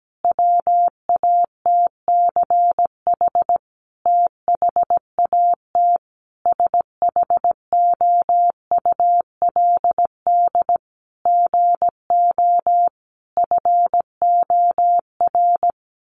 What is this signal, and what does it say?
Morse code, heard as a keyed tone, about 17 wpm, 700 Hz. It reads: WATCH THAT SHOULD GO FOR
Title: Sentences from Top 1000 Words 17wpm